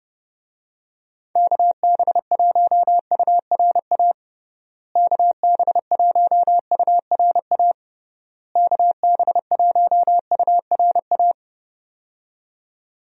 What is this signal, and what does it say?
Morse code, heard as a keyed tone, about 30 wpm, 700 Hz. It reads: KB1URA KB1URA KB1URA
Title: USA Call Signs - 6 Characters - Repeated 3x - 30wpm